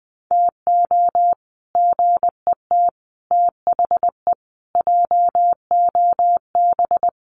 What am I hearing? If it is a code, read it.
Morse code, heard as a keyed tone, about 20 wpm, 700 Hz. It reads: TO GET THE JOB
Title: Sets of 5 Words Encore2 20wpm